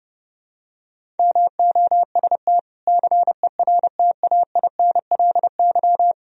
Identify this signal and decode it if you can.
Morse code, heard as a keyed tone, about 29 wpm, 700 Hz. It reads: MOST CERTAINLY
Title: Sentences from Top 1000 Words - Review Part 1 - 30wpm